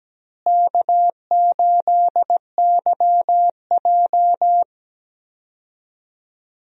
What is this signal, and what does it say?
Morse code, heard as a keyed tone, about 17 wpm, 700 Hz. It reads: K8YJ
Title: USA Call Signs - 4 Characters - Repeated 3x - 17wpm